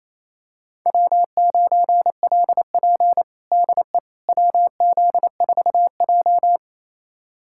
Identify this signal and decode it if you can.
Morse code, heard as a keyed tone, about 28 wpm, 690 Hz. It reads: W9LP DE WZ4J
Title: USA Callsign DE Callsign - 4 Characters - 28wpm